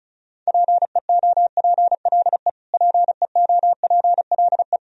Morse code, heard as a keyed tone, about 34 wpm, 700 Hz. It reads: PEOPLE PEOPLE
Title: Top 100 Words Repeated 35wpm